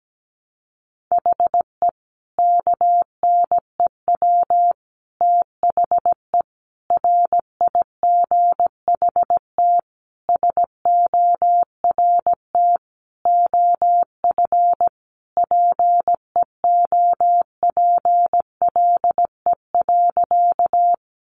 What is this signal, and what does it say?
Morse code, heard as a keyed tone, about 17 wpm, 700 Hz. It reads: HE KNEW THE RIGHT SORT OF PEOPLE.
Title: Sentences from Top 1300 Words 17wpm